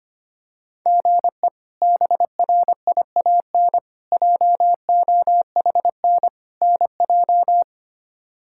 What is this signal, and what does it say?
Morse code, heard as a keyed tone, about 25 wpm, 700 Hz. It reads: GE BRIAN JOHN NJ
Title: K1USN SST QSO Practice - 25wpm